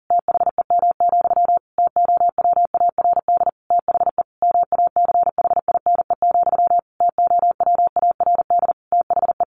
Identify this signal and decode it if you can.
Morse code, heard as a keyed tone, about 40 wpm, 700 Hz. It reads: THEM, TOWARD THE MACHINE, TOWARD THE
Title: A Sound Of Thunder 40wpm